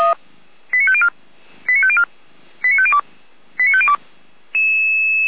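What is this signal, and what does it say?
Five-tone selective call (ZVEI), heard as quick runs of five tones, tones of about 70 ms. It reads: 89504 89504 89502 89502
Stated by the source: ZVEI five-tone selective call received on an SDR, recorded by Feuerwehr Orlamünde.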